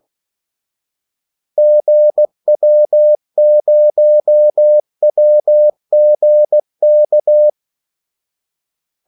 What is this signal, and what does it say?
Morse code, heard as a keyed tone, about 16 wpm, 600 Hz. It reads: GW0WGK
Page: Kaldesignaler 16 wpm